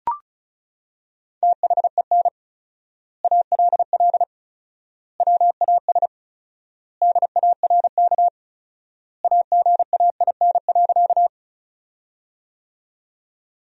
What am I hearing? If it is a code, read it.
Morse code, heard as a keyed tone, about 36 wpm, 700 Hz. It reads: THEN ALL WAS DARK AGAIN.
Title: 4x Word-Spacing - Sentences from Top 500 Words - 35wpm